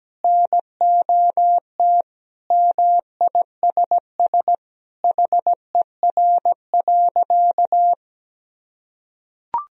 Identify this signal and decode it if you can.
Morse code, heard as a keyed tone, about 17 wpm, 700 Hz. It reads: NOT MISS HER.
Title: Sentences from Top 1000 Words - Review Part 3 - 17wpm